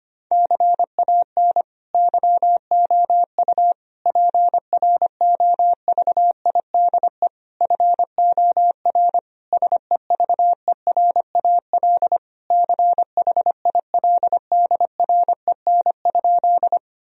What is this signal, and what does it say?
Morse code, heard as a keyed tone, about 25 wpm, 700 Hz. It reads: CAN YOU PROVIDE FOR SEVERAL CHILDREN?
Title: Sentences from Top 1000 Words - Review Part 2 - 25wpm